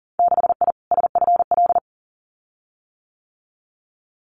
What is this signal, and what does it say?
Morse code, heard as a keyed tone, about 40 wpm, 700 Hz. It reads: 6I SFL
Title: Winter Field Day Exchanges 40wpm